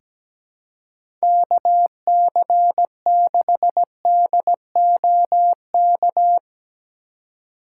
Morse code, heard as a keyed tone, about 17 wpm, 700 Hz. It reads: KC6DOK
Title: USA Call Signs - 6 Characters - 4x Character Spacing - 17wpm